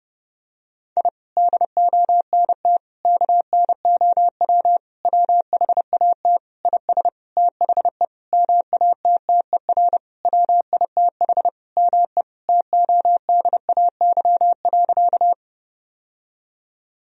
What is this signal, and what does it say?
Morse code, heard as a keyed tone, about 30 wpm, 700 Hz. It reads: I DONT KNOW WHAT IS THE MATTER WITH ME TODAY.
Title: Sentences from Top 1000 Words - Review Part 4 - 30wpm